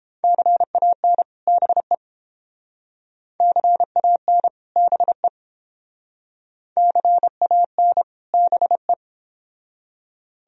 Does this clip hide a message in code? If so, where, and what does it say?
Yes: Morse code from 0.2 to 8.9 s, heard as a keyed tone, about 30 wpm, 700 Hz. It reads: CAN BE CAN BE CAN BE